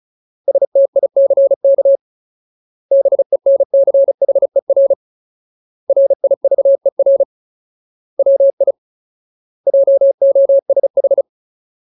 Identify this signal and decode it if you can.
Morse code, heard as a keyed tone, about 34 wpm, 550 Hz. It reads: STICK BENCHER RIVER WI JOSH
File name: Ultimate Word List / 35WPM Ultimate